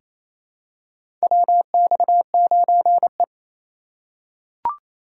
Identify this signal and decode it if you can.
Morse code, heard as a keyed tone, about 28 wpm, 700 Hz. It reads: WX9E
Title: USA Call Signs - 4x Character Spacing - 28wpm